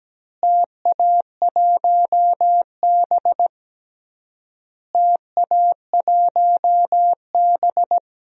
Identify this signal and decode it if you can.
Morse code, heard as a keyed tone, about 17 wpm, 700 Hz. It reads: TA1B TA1B
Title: DXCC-100 Call Signs - 4 Characters - Repeated 3x - 17wpm